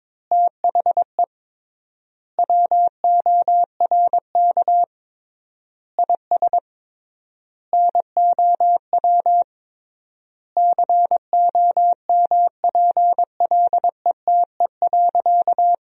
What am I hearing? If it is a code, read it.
Morse code, heard as a keyed tone, about 22 wpm, 700 Hz. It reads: THE WORK IS NOW COMPLETE.
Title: 3x Word-Spacing - Sentences from Top 400 Words - 22wpm